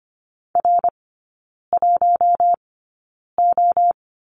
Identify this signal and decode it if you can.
Morse code, heard as a keyed tone, about 25 wpm, spaced as for 9 wpm, 700 Hz. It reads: R1O
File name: course / 25:10wpm / 093 Lesson 16e - Letter c - Three Characters 25wpm